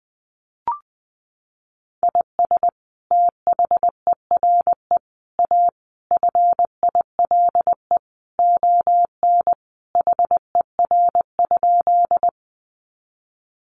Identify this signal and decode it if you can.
Morse code, heard as a keyed tone, about 20 wpm, 700 Hz. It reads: IS THERE A FILE ON HER?
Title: Sentences from Top 1200 Words 20wpm